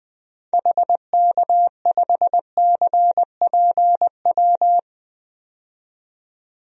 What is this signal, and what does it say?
Morse code, heard as a keyed tone, about 20 wpm, 700 Hz. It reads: HK5CPW
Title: DXCC-100 Call Signs - 6 Characters - 20wpm